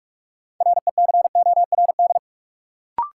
Morse code, heard as a keyed tone, about 46 wpm, 700 Hz. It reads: RECORD
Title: Sentences from Top 500 Words - New Word Drill - 45wpm